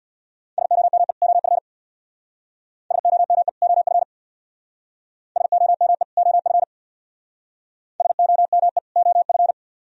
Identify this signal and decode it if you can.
Morse code, heard as a keyed tone, about 62 wpm, 700 Hz. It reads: SOME OF SOME OF SOME OF SOME OF
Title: Speed-Racing - Sets of 2 Words 50wpm